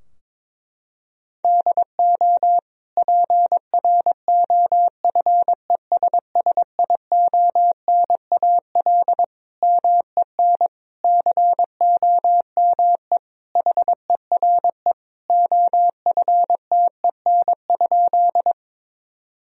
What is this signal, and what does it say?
Morse code, heard as a keyed tone, about 22 wpm, 700 Hz. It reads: DO PROFESSIONAL MEN COME HERE OFTEN?
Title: Sentences from Top 1000 Words - Review Part 4 - 22wpm